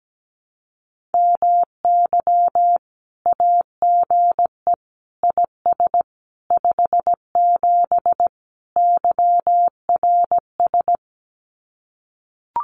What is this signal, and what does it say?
Morse code, heard as a keyed tone, about 17 wpm, 700 Hz. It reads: MY AGE IS 57 YRS
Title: Common QSO Phrases Encore 17wpm